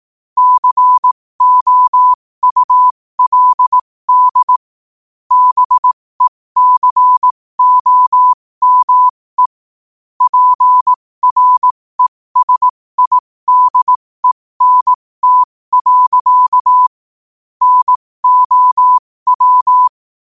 Morse code, heard as a keyed tone, about 18 wpm, spaced as for 15 wpm, 1000 Hz. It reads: COULD BECOME PRESIDENT. NOW
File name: 15wpm